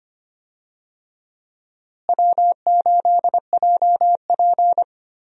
Morse code, heard as a keyed tone, about 25 wpm, 700 Hz. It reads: W8JP
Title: USA Call Signs - 4 Characters - Repeated 3x - 25wpm